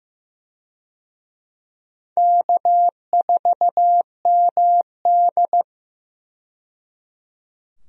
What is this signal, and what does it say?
Morse code, heard as a keyed tone, about 15 wpm, 700 Hz. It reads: K4MD
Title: USA Call Signs - 4 Characters - 15wpm